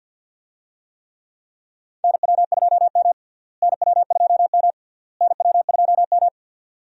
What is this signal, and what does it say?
Morse code, heard as a keyed tone, about 46 wpm, 700 Hz. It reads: NW2M NW2M NW2M
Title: USA Call Signs - 4 Characters - Repeated 3x - 50wpm